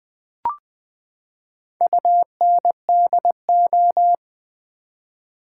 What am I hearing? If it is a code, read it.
Morse code, heard as a keyed tone, about 20 wpm, 700 Hz. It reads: UNDO
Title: Build A Word - 4 Letters At A Time - 11-Letter Words - 20wpm